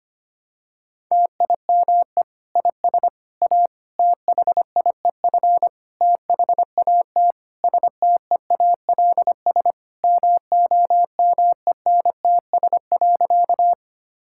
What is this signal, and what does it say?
Morse code, heard as a keyed tone, about 25 wpm, 700 Hz. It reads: TIME IS A THIEF THAT STEALS MOMENTS.
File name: practice / 25wpm / Difficult Sentences with Hints - 25wpm